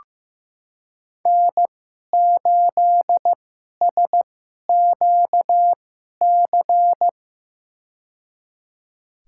Morse code, heard as a keyed tone, about 15 wpm, spaced as for 11 wpm, 700 Hz. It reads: N8SQC
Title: USA Call Signs - 5 Characters - 2x Character Spacing - 15wpm